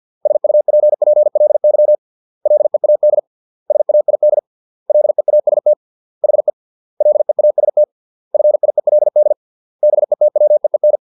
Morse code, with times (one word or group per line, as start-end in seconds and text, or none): 0.3-1.9 SUPPLY
2.5-3.2 LEAD
3.7-4.4 SAID
4.9-5.7 LEAST
6.2-6.5 HE
7.0-7.8 LEAST
8.3-9.3 FIELD
9.8-11.0 BETWEEN